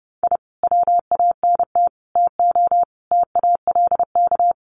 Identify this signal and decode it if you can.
Morse code, heard as a keyed tone, about 30 wpm, 700 Hz. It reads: I WANT TO TALK
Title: Sets of 5 Words Encore 30wpm